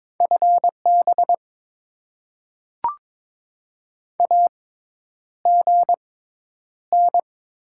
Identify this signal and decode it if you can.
Morse code, heard as a keyed tone, about 22 wpm, spaced as for 10 wpm, 700 Hz. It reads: FB AGN
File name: Top 20 QSO Elements - 6x Character Spacing - 22wpm